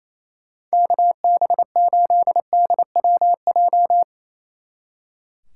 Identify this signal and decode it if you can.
Morse code, heard as a keyed tone, about 34 wpm, 700 Hz. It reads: KB8DWJ
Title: Speed Racing - USA 2x3 Call Signs 28wpm